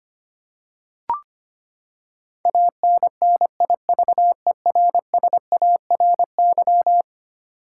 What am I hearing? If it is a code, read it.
Morse code, heard as a keyed tone, about 25 wpm, 700 Hz. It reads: ANNIVERSARY
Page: morse-code-ninja-practice-25wpm